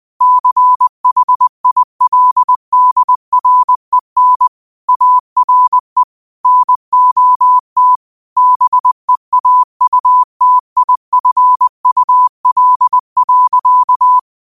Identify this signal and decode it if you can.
Morse code, heard as a keyed tone, about 20 wpm, 1000 Hz. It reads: CHILDREN ARE NOT BEAUTIFUL.
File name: qotdfiles / 20wpm